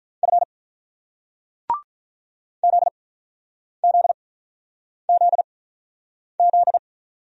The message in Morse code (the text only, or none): F ZZZZ